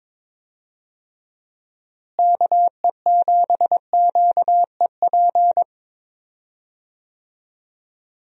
Morse code, heard as a keyed tone, about 21 wpm, 700 Hz. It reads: KE7QEP
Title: USA Call Signs - 6 Characters - Repeated 3x - 22wpm